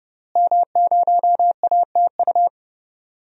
Morse code, heard as a keyed tone, about 30 wpm, 700 Hz. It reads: M0ATU
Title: DXCC-100 Call Signs - 5 Characters - Repeated 3x - 30wpm